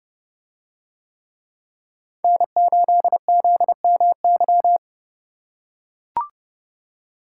Morse code, heard as a keyed tone, about 30 wpm, 700 Hz. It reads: N8ZMY